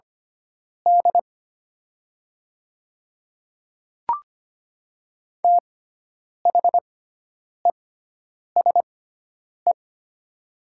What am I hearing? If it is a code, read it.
Morse code, heard as a keyed tone, about 25 wpm, spaced as for 9 wpm, 690 Hz. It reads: D THESE
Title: 6x Character Spacing - Top 200 Words - 25wpm